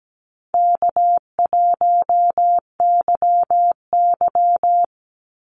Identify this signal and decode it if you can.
Morse code, heard as a keyed tone, about 17 wpm, 700 Hz. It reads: K1YY
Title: Easy Call Signs Repeated 2x - 17wpm